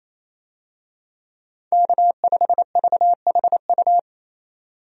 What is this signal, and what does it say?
Morse code, heard as a keyed tone, about 28 wpm, 700 Hz. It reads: K5VHU
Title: USA Call Signs - 5 Characters - 28wpm